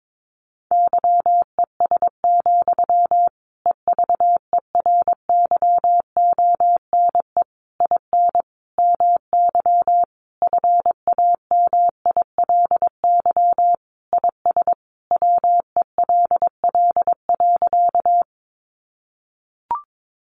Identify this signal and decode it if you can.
Morse code, heard as a keyed tone, about 22 wpm, 700 Hz. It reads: YES, EVERYONE IN MY FAMILY IS WELL.
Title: Sentences from Top 800 Words 22wpm